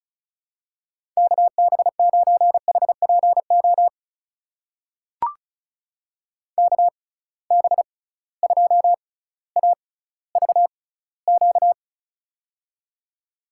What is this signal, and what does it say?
Morse code, heard as a keyed tone, about 36 wpm, spaced as for 18 wpm, 690 Hz. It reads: KB9HPO KB2AVQ
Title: USA Call Signs - 6 Characters - 6x Character Spacing - 35wpm